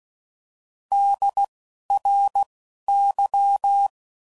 Morse code, heard as a keyed tone, about 16 wpm, 800 Hz. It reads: D R Y
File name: Prefixes / By First Letter / Ds